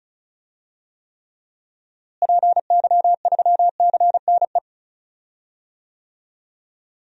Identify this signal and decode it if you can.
Morse code, heard as a keyed tone, about 34 wpm, 700 Hz. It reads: PY3CNE
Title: DXCC-100 Call Signs - 6 Characters - Repeated 3x - 35wpm